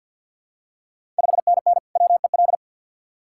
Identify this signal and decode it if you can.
Morse code, heard as a keyed tone, about 53 wpm, 690 Hz. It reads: 5NN WEL
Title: Ontario QSO Party Exchange 50wpm